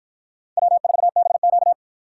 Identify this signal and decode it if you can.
Morse code, heard as a keyed tone, about 49 wpm, 700 Hz. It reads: W4BQ